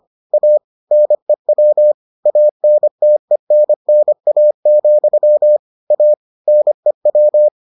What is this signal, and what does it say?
Morse code, heard as a keyed tone, about 25 wpm, 600 Hz. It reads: A NEW ANTENNA, A NEW